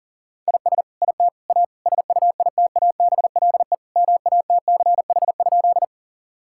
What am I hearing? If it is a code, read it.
Morse code, heard as a keyed tone, about 37 wpm, 700 Hz. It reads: IS IT A SUITABLE MATCH?
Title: Sentences from Top 1600 Words 40wpm